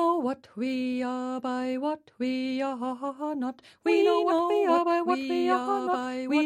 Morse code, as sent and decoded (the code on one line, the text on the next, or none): none